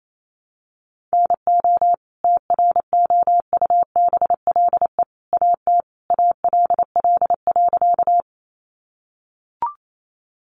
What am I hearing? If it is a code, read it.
Morse code, heard as a keyed tone, about 28 wpm, 700 Hz. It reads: NO TROUBLE AT ALL.
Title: Sentences from Top 1200 Words 28wpm